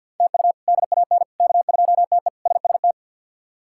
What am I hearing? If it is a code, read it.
Morse code, heard as a keyed tone, about 50 wpm, 700 Hz. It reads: TU DAN K2TE SST
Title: K1USN SST QSO Practice - 50wpm